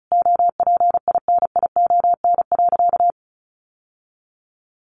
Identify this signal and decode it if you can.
Morse code, heard as a keyed tone, about 34 wpm, 700 Hz. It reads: OPINION.
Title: Sentences from Top 1200 Words 35wpm